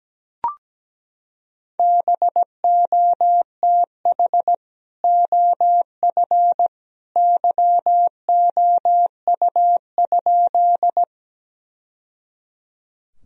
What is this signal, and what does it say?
Morse code, heard as a keyed tone, about 17 wpm, 700 Hz. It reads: BOTH OF YOU?
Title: Sentences from Top 1000 Words - Review Part 1 - 17wpm